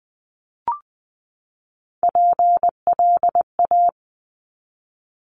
Morse code, heard as a keyed tone, about 20 wpm, 700 Hz. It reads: PLA
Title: Top Three Letter Combinations in Words 20wpm